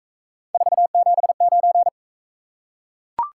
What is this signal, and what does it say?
Morse code, heard as a keyed tone, about 38 wpm, 700 Hz. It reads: 479